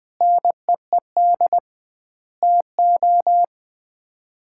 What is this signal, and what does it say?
Morse code, heard as a keyed tone, about 20 wpm, 700 Hz. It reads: NEED TO
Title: 2x Word-Spacing - Sets of 5 Words - 20wpm